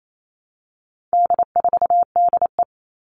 Morse code, heard as a keyed tone, about 28 wpm, 700 Hz. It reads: D4DE
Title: All Cut Numbers - 4-Digit 28wpm